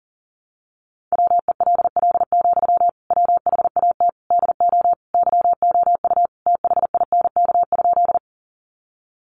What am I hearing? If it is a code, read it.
Morse code, heard as a keyed tone, about 40 wpm, 700 Hz. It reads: WELL, WHAT DO YOU THINK?
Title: Sentences from Top 200 Words 40wpm